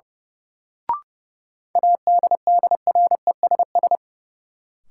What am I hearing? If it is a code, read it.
Morse code, heard as a keyed tone, about 30 wpm, 700 Hz. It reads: ADDRESS
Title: Sentences from Top 1000 Words - New Word Drill - 30wpm